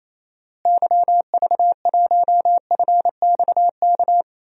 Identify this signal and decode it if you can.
Morse code, heard as a keyed tone, about 27 wpm, 700 Hz. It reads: YV1FXK
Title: DXCC-100 Call Signs - 6 Characters - Repeated 3x - 28wpm